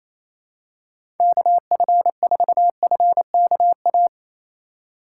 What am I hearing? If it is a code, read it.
Morse code, heard as a keyed tone, about 28 wpm, 700 Hz. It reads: KF4FKA
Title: USA Call Signs - 2x Character Spacing - 28wpm